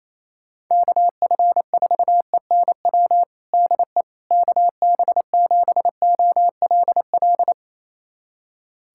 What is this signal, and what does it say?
Morse code, heard as a keyed tone, about 28 wpm, 700 Hz. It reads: KF4ENW DE KB7OLL
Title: USA Callsign DE Callsign - 6 Characters - 28wpm